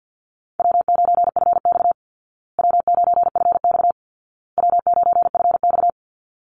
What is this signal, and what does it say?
Morse code, heard as a keyed tone, about 50 wpm, 700 Hz. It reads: W9FX W9FX W9FX
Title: Easy Call Signs Repeated 3x - 50wpm